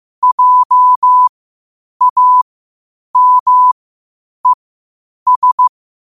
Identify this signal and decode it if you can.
Morse code, heard as a keyed tone, about 15 wpm, spaced as for 8 wpm, 1000 Hz. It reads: JAMES
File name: qotdfiles / 8wpm